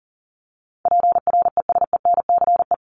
Morse code, heard as a keyed tone, about 40 wpm, 700 Hz. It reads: PRESENCE
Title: Top 1751-2000 Words 40wpm